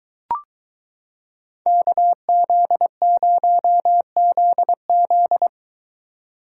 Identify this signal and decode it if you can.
Morse code, heard as a keyed tone, about 22 wpm, 700 Hz. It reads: KZ0ZZ